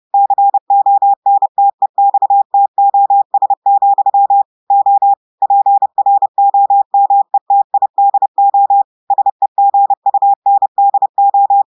Morse code, heard as a keyed tone, about 30 wpm, 800 Hz. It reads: CONTEXTOS, O PROMETIDO SEGUNDO
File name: audios / portugais / portugais texte en clair 1 vitesse 30 mots minutes